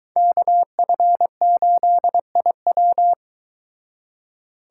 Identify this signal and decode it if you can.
Morse code, heard as a keyed tone, about 22 wpm, 700 Hz. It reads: KF8IW